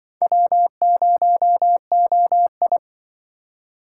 Morse code, heard as a keyed tone, about 24 wpm, 700 Hz. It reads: W0OI
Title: Speed Racing - USA Call Signs 20wpm